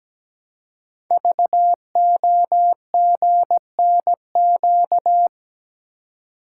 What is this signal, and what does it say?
Morse code, heard as a keyed tone, about 17 wpm, 700 Hz. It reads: VOGNQ
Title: Code Groups - 5 Letters - 17wpm